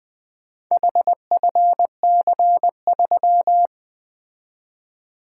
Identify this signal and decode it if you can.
Morse code, heard as a keyed tone, about 22 wpm, 700 Hz. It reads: HFC3
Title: Speed-Racing - Four Letter-Number Combinations - 20wpm